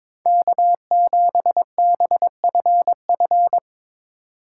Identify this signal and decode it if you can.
Morse code, heard as a keyed tone, about 22 wpm, 700 Hz. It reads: K7BFF
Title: USA Call Signs - 4x Character Spacing - 22wpm